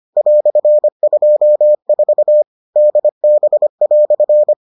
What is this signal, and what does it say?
Morse code, heard as a keyed tone, about 25 wpm, 600 Hz. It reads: "24 DB"